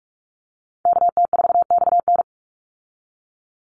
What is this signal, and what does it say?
Morse code, heard as a keyed tone, about 44 wpm, 700 Hz. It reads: KT4XN
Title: Call Signs Repeated 3x - 45wpm